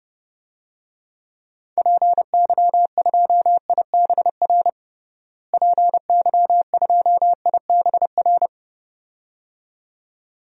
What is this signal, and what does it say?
Morse code, heard as a keyed tone, about 30 wpm, 700 Hz. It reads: PY2IBR PY2IBR
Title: DXCC-100 Call Signs - 6 Characters - Repeated 2x - 30wpm